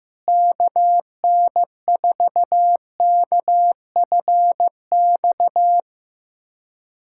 Morse code, heard as a keyed tone, about 15 wpm, 700 Hz. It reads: KN4KFX
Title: USA Callsign DE Callsign - 6 Characters - 15wpm